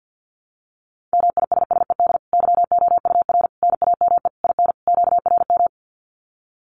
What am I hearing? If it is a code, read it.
Morse code, heard as a keyed tone, about 50 wpm, 700 Hz. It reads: MISSED YOUR NAME IN QRM
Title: Common QSO Phrases 50wpm